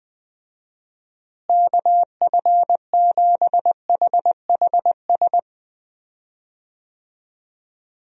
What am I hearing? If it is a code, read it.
Morse code, heard as a keyed tone, about 20 wpm, 700 Hz. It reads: KF7HHS
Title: USA Call Signs - 6 Characters - Repeated 2x - 20wpm